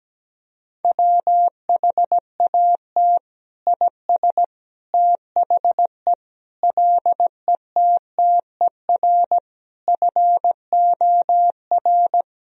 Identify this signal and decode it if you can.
Morse code, heard as a keyed tone, about 17 wpm, 700 Hz. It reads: WHAT IS THE LETTER FOR
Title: Sentences from Top 1000 Words - Review Part 2 - 17wpm